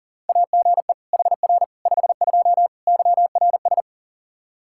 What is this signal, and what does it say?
Morse code, heard as a keyed tone, about 40 wpm, 700 Hz. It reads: AGE HR 52 YRS
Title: Practice QSO - Exchange 3 - More Info - 40wpm